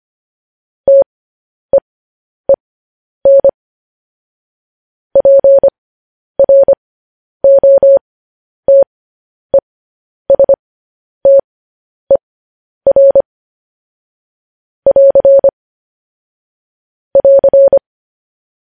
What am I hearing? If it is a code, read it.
Morse code, heard as a keyed tone, about 25 wpm, spaced as for 10 wpm, 550 Hz. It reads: TEEN PROTESTER <AR> <AR>